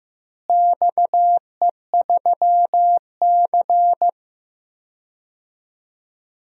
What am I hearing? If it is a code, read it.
Morse code, heard as a keyed tone, about 15 wpm, 700 Hz. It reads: XE3C
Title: DXCC-100 Call Signs - 4 Characters - Repeated 2x - 15wpm